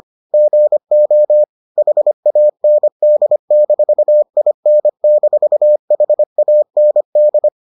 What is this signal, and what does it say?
Morse code, heard as a keyed tone, about 25 wpm, 600 Hz. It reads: GO HAND-IN-HAND